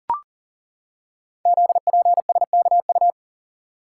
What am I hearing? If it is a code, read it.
Morse code, heard as a keyed tone, about 39 wpm, 700 Hz. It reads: ZPSKU
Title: Code Groups - 5 Letter:Numbers - 40wpm